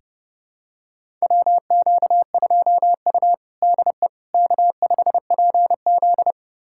Morse code, heard as a keyed tone, about 30 wpm, 700 Hz. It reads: WQ2U DE K5PZ
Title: USA Callsign DE Callsign - 4 Characters - 30wpm